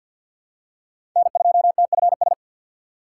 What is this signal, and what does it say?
Morse code, heard as a keyed tone, about 51 wpm, 690 Hz. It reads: N2TFS
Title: USA Call Signs - 5 Characters - 3x Character Spacing - 50wpm